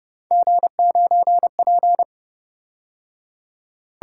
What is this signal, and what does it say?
Morse code, heard as a keyed tone, about 29 wpm, 700 Hz. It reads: G9P